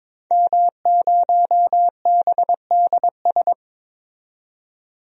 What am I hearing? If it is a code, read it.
Morse code, heard as a keyed tone, about 22 wpm, 700 Hz. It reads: M0BDS
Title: DXCC-100 Call Signs - 5 Characters - 22wpm